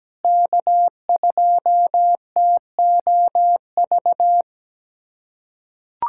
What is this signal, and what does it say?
Morse code, heard as a keyed tone, about 17 wpm, 700 Hz. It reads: K2TOV
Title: USA Callsign DE Callsign 17wpm